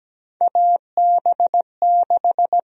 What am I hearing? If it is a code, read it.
Morse code, heard as a keyed tone, about 17 wpm, 700 Hz. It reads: AB6
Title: USA Call Sign Prefixes - Repeated 2x - 17wpm